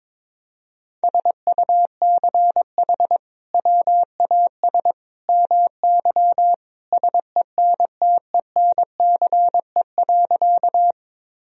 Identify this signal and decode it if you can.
Morse code, heard as a keyed tone, about 22 wpm, 700 Hz. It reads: SUCH WAS MY SENTENCE.
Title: Sentences from Top 1000 Words - Review Part 1 - 22wpm